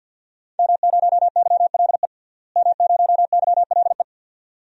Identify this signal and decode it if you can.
Morse code, heard as a keyed tone, about 47 wpm, 700 Hz. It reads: M0YLE M0YLE
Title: DXCC-100 Call Signs - 5 Characters - Repeated 2x - 50wpm